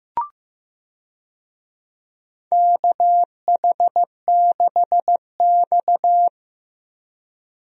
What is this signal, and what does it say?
Morse code, heard as a keyed tone, about 15 wpm, 700 Hz. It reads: KH6X
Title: DXCC-100 Call Signs - 4 Characters - 15wpm